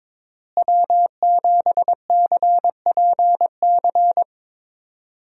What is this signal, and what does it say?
Morse code, heard as a keyed tone, about 22 wpm, 700 Hz. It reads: W7CPC
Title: USA Call Signs - 5 Characters - Repeated 2x - 22wpm